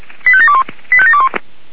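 Five-tone selective call (ZVEI), heard as quick runs of five tones, tones of about 70 ms. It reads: 85721 85721
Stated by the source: ZVEI five-tone selective call received with an SDR.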